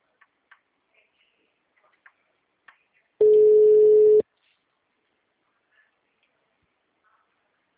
Ringback tone (Europe, 425 Hz).